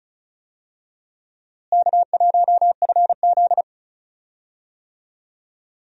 Morse code, heard as a keyed tone, about 34 wpm, 700 Hz. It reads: K1FZ
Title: USA Call Signs - 4 Characters - Repeated 3x - 35wpm